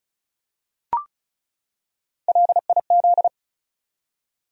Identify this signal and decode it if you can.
Morse code, heard as a keyed tone, about 34 wpm, 700 Hz. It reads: LIZ
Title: USA Common Names - Max 4 Letters - 35wpm